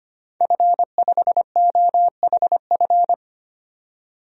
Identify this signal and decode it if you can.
Morse code, heard as a keyed tone, about 25 wpm, 700 Hz. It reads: F5OHF
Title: DXCC-100 Call Signs - 5 Characters - 25wpm